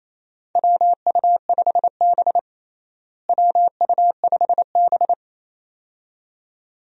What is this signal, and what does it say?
Morse code, heard as a keyed tone, about 28 wpm, 700 Hz. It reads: WU5B WU5B
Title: USA Call Signs - 4 Characters - Repeated 2x - 28wpm